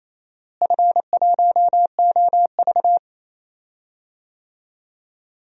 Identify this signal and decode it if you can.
Morse code, heard as a keyed tone, about 28 wpm, 700 Hz. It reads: F1OV
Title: DXCC-100 Call Signs - 4 Characters - Repeated 3x - 28wpm